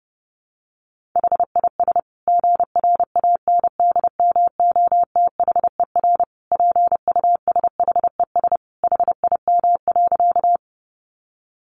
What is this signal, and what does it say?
Morse code, heard as a keyed tone, about 30 wpm, 700 Hz. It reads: HIS GRANDMOTHER PUSHES HIM.
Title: Additional Easy Phrases 30wpm